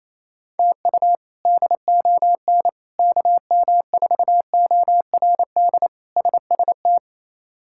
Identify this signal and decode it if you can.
Morse code, heard as a keyed tone, about 28 wpm, 700 Hz. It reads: TU DON KM4ORD SST
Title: K1USN SST QSO Practice - 28wpm